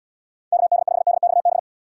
Morse code, heard as a keyed tone, about 78 wpm, 700 Hz. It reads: KD5DXB